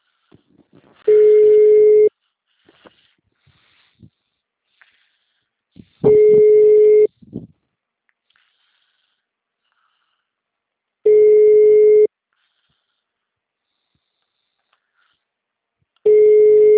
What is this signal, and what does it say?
Ringback tone (Europe, 425 Hz).
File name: out- -2007- - - .436